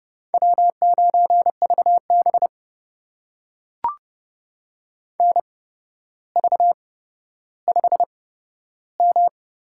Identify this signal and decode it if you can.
Morse code, heard as a keyed tone, about 30 wpm, spaced as for 13 wpm, 700 Hz. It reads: W9VB NV5M
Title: USA Call Signs - 4 Characters - 8x Character Spacing - 30wpm